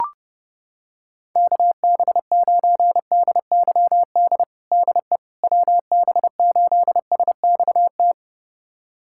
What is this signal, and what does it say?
Morse code, heard as a keyed tone, about 30 wpm, 700 Hz. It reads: KB9DYD DE WB8SXT
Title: USA Callsign DE Callsign - 6 Characters - 30wpm